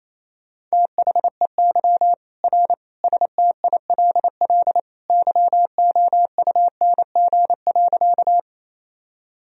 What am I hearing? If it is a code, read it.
Morse code, heard as a keyed tone, about 28 wpm, 700 Hz. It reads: THEY R STILL YOUNG.
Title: Sentences from Top 400 Words and Abbreviations 28wpm